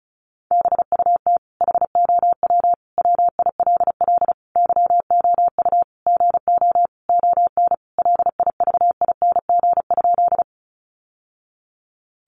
Morse code, heard as a keyed tone, about 34 wpm, 700 Hz. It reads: BUT HOW WILL YOU GO ON LIVING?
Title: Sentences from Top 1100 Words 35wpm